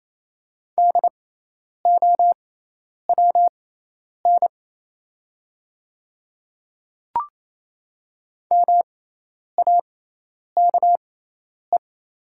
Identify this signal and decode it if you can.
Morse code, heard as a keyed tone, about 28 wpm, spaced as for 10 wpm, 700 Hz. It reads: DOWN MAKE
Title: 6x Character Spacing - Top 100 Words - 28wpm